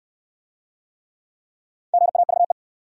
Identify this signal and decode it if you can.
Morse code, heard as a keyed tone, about 68 wpm, 700 Hz. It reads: GAVE